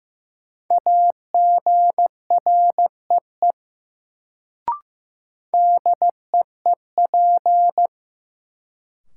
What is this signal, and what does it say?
Morse code, heard as a keyed tone, about 15 wpm, 700 Hz. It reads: AGREE DEEP
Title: Sets of 5 Words Encore - New Word Drill - 15wpm